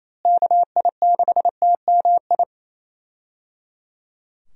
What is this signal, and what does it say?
Morse code, heard as a keyed tone, about 28 wpm, 700 Hz. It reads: KI6TMI
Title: USA Call Signs - 6 Characters - Repeated 3x - 28wpm